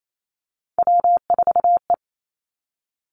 Morse code, heard as a keyed tone, about 28 wpm, 700 Hz. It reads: W4E
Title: Three Challenging Letter-Number Combinations 28wpm